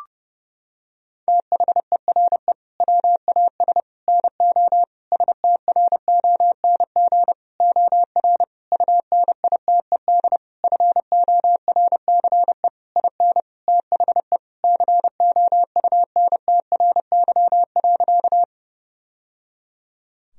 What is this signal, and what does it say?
Morse code, heard as a keyed tone, about 32 wpm, 700 Hz. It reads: THERE WAS NO STRONG OR UNITED FORCE IN THE COUNTRY.
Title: Sentences from Top 1000 Words - Review Part 4 - 30wpm